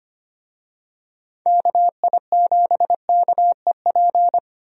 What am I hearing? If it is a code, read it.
Morse code, heard as a keyed tone, about 25 wpm, 700 Hz. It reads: KI7KEP
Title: USA Call Signs - 6 Characters - 25wpm